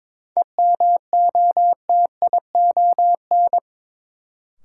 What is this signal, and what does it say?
Morse code, heard as a keyed tone, about 22 wpm, 700 Hz. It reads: EMOTION
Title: Sentences from Top 1600 Words - New Word Drill Alt - 22wpm